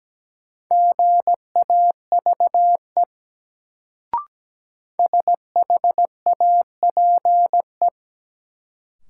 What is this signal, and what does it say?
Morse code, heard as a keyed tone, about 17 wpm, 700 Hz. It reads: GAVE SHAPE
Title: Sentences from Top 500 Words - New Word Drill - 17wpm